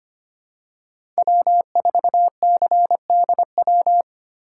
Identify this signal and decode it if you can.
Morse code, heard as a keyed tone, about 25 wpm, 700 Hz. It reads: W4CDW
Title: USA Call Signs - 5 Characters - 2x Character Spacing - 25wpm